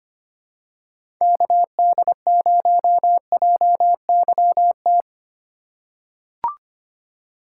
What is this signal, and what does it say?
Morse code, heard as a keyed tone, about 25 wpm, 700 Hz. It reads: KD0JYT
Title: USA Call Signs - 6 Characters - 2x Character Spacing - 25wpm